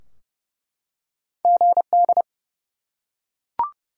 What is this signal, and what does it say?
Morse code, heard as a keyed tone, about 29 wpm, 700 Hz. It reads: GD
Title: Top 90 QSO Elements 30wpm